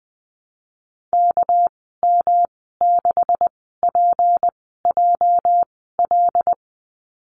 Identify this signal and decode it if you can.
Morse code, heard as a keyed tone, about 20 wpm, spaced as for 14 wpm, 700 Hz. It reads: KM6PJL
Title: 207 Lesson 33g - Letter j - Call Signs 20wpm